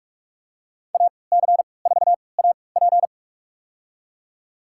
Morse code, heard as a keyed tone, about 46 wpm, spaced as for 26 wpm, 700 Hz. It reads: AC4AP
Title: USA Call Signs - 5 Characters - 3x Character Spacing - 45wpm